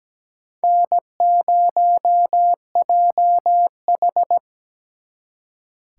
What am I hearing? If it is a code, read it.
Morse code, heard as a keyed tone, about 20 wpm, 700 Hz. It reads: N0JH
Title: Speed Racing - USA Call Signs 17wpm